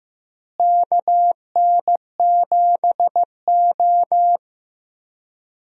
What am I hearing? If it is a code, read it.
Morse code, heard as a keyed tone, about 15 wpm, 700 Hz. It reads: KN7O
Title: USA Call Signs - 4 Characters - Repeated 2x - 15wpm